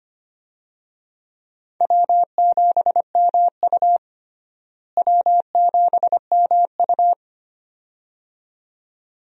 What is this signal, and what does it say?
Morse code, heard as a keyed tone, about 25 wpm, 700 Hz. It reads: W7MU W7MU
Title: USA Call Signs - 4 Characters - Repeated 2x - 25wpm